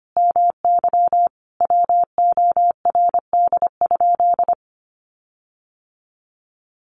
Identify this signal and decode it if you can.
Morse code, heard as a keyed tone, about 25 wpm, 700 Hz. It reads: MY WORD?
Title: Sentences from Top 1500 Words 25wpm